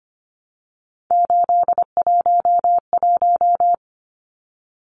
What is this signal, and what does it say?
Morse code, heard as a keyed tone, about 25 wpm, 700 Hz. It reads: 811
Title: Numbers - 3-Digit 25wpm